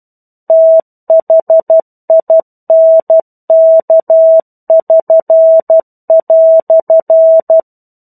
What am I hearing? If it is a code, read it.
Morse code, heard as a keyed tone, about 12 wpm, 650 Hz. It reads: THINK<SN>"
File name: 01-Chapter-01 12 WPM